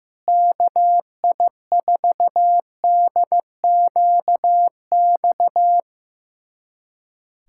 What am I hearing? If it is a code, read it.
Morse code, heard as a keyed tone, about 16 wpm, 700 Hz. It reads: KI4DQX
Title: USA Callsign DE Callsign 15wpm